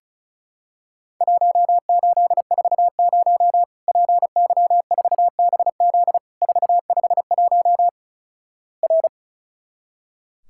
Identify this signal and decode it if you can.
Morse code, heard as a keyed tone, about 36 wpm, 700 Hz. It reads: 1840 PY4BZ 451 R
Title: Worked All Europe DX Contest - QTC Exchange - 35wpm